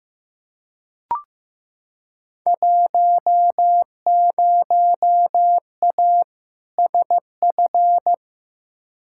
Morse code, heard as a keyed tone, about 15 wpm, 700 Hz. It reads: 10A SF
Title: Field Day Exchanges Encore - 15wpm